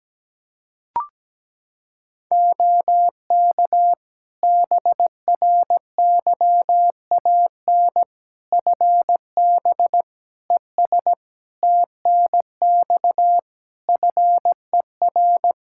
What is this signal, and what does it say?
Morse code, heard as a keyed tone, about 17 wpm, 700 Hz. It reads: OK BRYAN FB ES TNX FER
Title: Practice QSO - Exchange 2 - Info - 17wpm